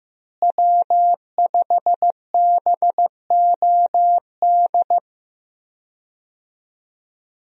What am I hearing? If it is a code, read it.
Morse code, heard as a keyed tone, about 15 wpm, 700 Hz. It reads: W5BOD
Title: USA Call Signs - 5 Characters - Repeated 3x - 15wpm